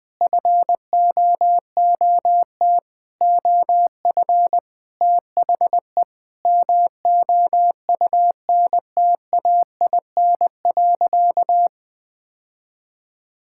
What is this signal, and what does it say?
Morse code, heard as a keyed tone, about 20 wpm, 700 Hz. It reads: FOOT OF THE MOUNTAIN.
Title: Sentences from Top 1000 Words - Review Part 5 - 20wpm